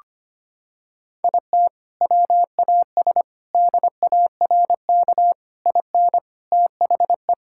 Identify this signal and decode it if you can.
Morse code, heard as a keyed tone, about 25 wpm, 700 Hz. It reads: IT WAS DARK IN THE
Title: Sentences from Top 1600 Words 25wpm